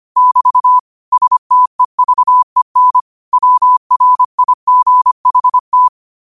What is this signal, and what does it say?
Morse code, heard as a keyed tone, about 25 wpm, 1000 Hz. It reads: <BT> STEVEN WRIGHT